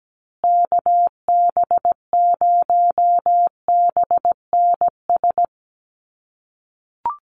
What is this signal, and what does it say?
Morse code, heard as a keyed tone, about 17 wpm, 700 Hz. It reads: KB0BNS
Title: USA Call Signs Repeated 2x 17wpm